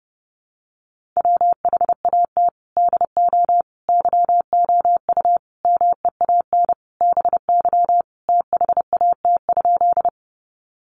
Morse code, heard as a keyed tone, about 30 wpm, 700 Hz. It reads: WHAT DO YOU MEAN BY THAT?
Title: Sentences from Top 200 Words 30wpm